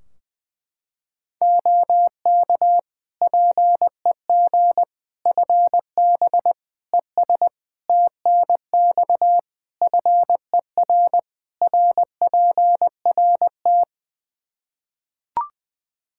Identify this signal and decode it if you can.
Morse code, heard as a keyed tone, about 21 wpm, 700 Hz. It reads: OK PEG FB ES TNX FER RPRT
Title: Practice QSO - Exchange 2 - Info - 20wpm